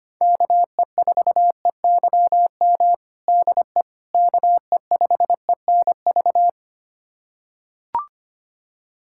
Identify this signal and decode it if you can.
Morse code, heard as a keyed tone, about 25 wpm, 700 Hz. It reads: KE4EYM DE KE5ENV
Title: USA Callsign DE Callsign - 6 Characters - 25wpm